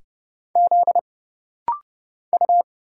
Morse code, heard as a keyed tone, about 30 wpm, 700 Hz. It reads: Z U
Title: Single Letters - Rapid-Fire 30wpm